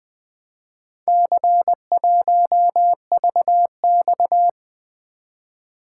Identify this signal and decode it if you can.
Morse code, heard as a keyed tone, about 20 wpm, 700 Hz. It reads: C1VX
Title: Four Letter-Number Combinations - 20wpm